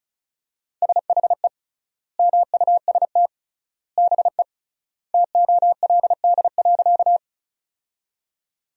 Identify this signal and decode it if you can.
Morse code, heard as a keyed tone, about 36 wpm, 700 Hz. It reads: SHE MUST BE TOLD.
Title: 3x Word-Spacing - Sentences from Top 400 Words - 35wpm